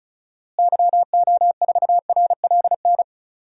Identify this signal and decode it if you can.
Morse code, heard as a keyed tone, about 34 wpm, 700 Hz. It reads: YO4RLN
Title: DXCC-100 Call Signs - 6 Characters - Repeated 2x - 35wpm